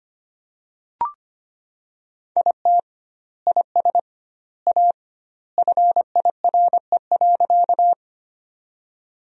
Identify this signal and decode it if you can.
Morse code, heard as a keyed tone, about 25 wpm, 700 Hz. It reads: IT IS A FIRE.
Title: 2x Word-Spacing - Sentences from Top 400 Words - 25wpm